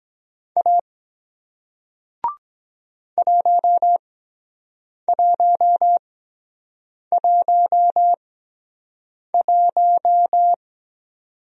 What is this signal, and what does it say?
Morse code, heard as a keyed tone, about 24 wpm, spaced as for 7 wpm, 700 Hz. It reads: A 1111